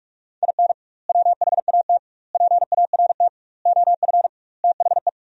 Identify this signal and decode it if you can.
Morse code, heard as a keyed tone, about 46 wpm, 700 Hz. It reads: IN WHAT PART OF THE
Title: 2x Word-Spacing - Sentences from Top 200 Words - 45wpm